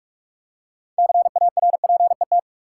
Morse code, heard as a keyed tone, about 43 wpm, 700 Hz. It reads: CARPET
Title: Six Letter Words - 45wpm